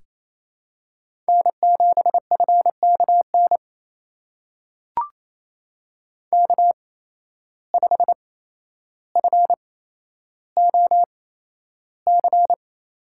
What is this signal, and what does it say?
Morse code, heard as a keyed tone, about 28 wpm, spaced as for 12 wpm, 700 Hz. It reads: N7FKN K5FOC
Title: USA Call Signs - 5 Characters - 8x Character Spacing - 28wpm